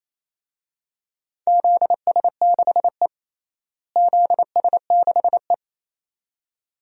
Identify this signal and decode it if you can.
Morse code, heard as a keyed tone, about 28 wpm, 700 Hz. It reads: ZS6E ZS6E
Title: DXCC-100 Call Signs - 4 Characters - Repeated 2x - 28wpm